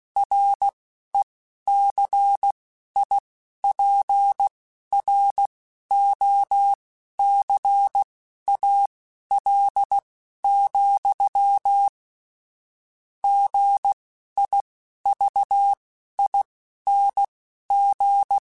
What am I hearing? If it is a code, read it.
Morse code, heard as a keyed tone, about 16 wpm, spaced as for 11 wpm, 800 Hz. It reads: RECIPROCAL, GIVING
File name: Technical Subjects / Technical Subjects-mp3s / XC @10